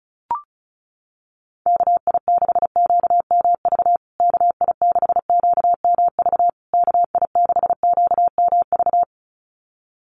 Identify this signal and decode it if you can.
Morse code, heard as a keyed tone, about 34 wpm, 700 Hz. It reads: KI6QMV KI6QMV KI6QMV
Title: USA Call Signs Repeated 3x 35wpm